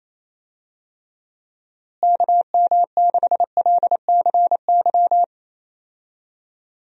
Morse code, heard as a keyed tone, about 28 wpm, 700 Hz. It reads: KM6LCY
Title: USA Call Signs - 6 Characters - Repeated 2x - 28wpm